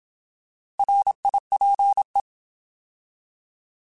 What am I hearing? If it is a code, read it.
Morse code, heard as a keyed tone, about 27 wpm, 800 Hz. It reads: RIPE
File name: Alphabetical / Alphabetical mp3s / Rs @25